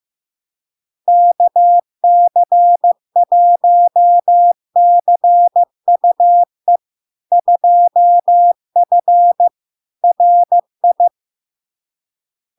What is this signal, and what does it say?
Morse code, heard as a keyed tone, about 15 wpm, 700 Hz. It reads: KC1CUE 2F RI
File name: LARK-FD-2022-Log-in-CW-15wpm-700-Hz